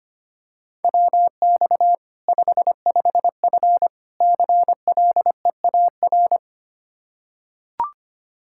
Morse code, heard as a keyed tone, about 25 wpm, 700 Hz. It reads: WX 55F CLEAR
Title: Practice QSO - Exchange 2 - Info - 25wpm